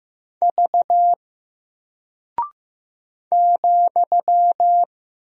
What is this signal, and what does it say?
Morse code, heard as a keyed tone, about 15 wpm, 700 Hz. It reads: V ,
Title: Single Character (Letter:Number:Punctuation:Prosign) 15wpm